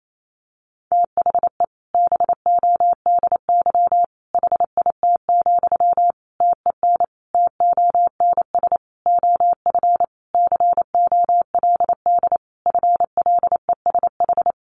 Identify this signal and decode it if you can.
Morse code, heard as a keyed tone, about 28 wpm, 700 Hz. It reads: THE BODY HIT, TEN TONS OF COLD FLESH
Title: A Sound Of Thunder 28wpm